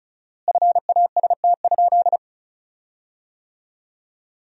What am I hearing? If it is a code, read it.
Morse code, heard as a keyed tone, about 34 wpm, 700 Hz. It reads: FAST?
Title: 4x Word-Spacing - Sentences from Top 400 Words - 35wpm